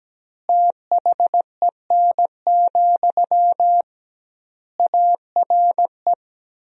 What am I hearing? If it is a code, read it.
Morse code, heard as a keyed tone, about 17 wpm, 700 Hz. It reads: THEN, ARE
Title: 2x Word-Spacing - Sentences from Top 100 Words - 17wpm